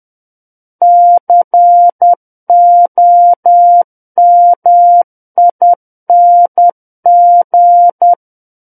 Morse code, heard as a keyed tone, about 10 wpm, 700 Hz. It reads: COMING
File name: Book One / 01-Chapter 1 - 10 WPM